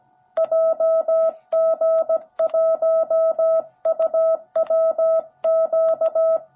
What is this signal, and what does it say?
Morse code, heard as a keyed tone, about 17 wpm, 650 Hz. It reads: JG1UWQ
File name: jh8cbh / 2021cw / cw17b